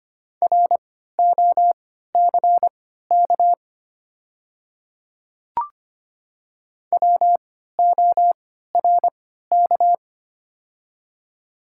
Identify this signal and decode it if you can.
Morse code, heard as a keyed tone, about 25 wpm, spaced as for 14 wpm, 700 Hz. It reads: ROCK WORK
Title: 3x Character Spacing - Top 500 Words - 25wpm